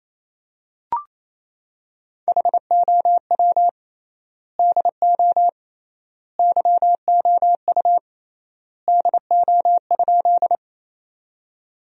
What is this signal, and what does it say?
Morse code, heard as a keyed tone, about 28 wpm, 700 Hz. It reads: HOW DO YOU DO?
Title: 3x Word-Spacing - Sentences from Top 100 Words - 28wpm